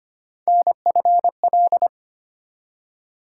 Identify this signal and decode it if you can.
Morse code, heard as a keyed tone, about 25 wpm, 700 Hz. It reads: NFL
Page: morse-code-ninja-practice-25wpm